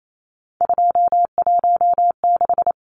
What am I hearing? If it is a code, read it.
Morse code, heard as a keyed tone, about 28 wpm, 700 Hz. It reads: 216
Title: Numbers - 3-Digit 28wpm